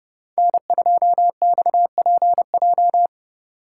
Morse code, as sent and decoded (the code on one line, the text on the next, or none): -. ..--- -..- .--. .---
N2XPJ